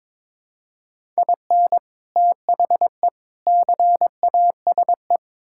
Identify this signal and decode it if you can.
Morse code, heard as a keyed tone, about 22 wpm, 700 Hz. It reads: IN THE CASE
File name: practice / 22wpm / Build A Phrase - 5-Word Phrases - 22wpm